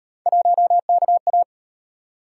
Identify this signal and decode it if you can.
Morse code, heard as a keyed tone, about 33 wpm, 700 Hz. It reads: 1KA